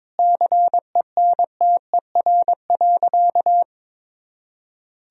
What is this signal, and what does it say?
Morse code, heard as a keyed tone, about 22 wpm, 700 Hz. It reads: CENTER.
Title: Sentences from Top 1000 Words - Review Part 2 - 22wpm